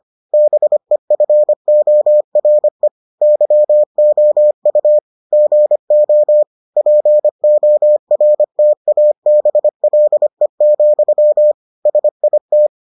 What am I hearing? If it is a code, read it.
Morse code, heard as a keyed tone, about 25 wpm, 600 Hz. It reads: BEFORE YOU GO PORTABLE, SIT